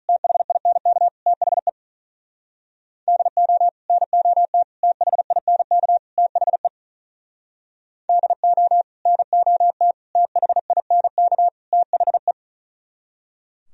Morse code, heard as a keyed tone, about 45 wpm, 700 Hz. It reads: THINK THE DO NOT THINK THE DO NOT THINK THE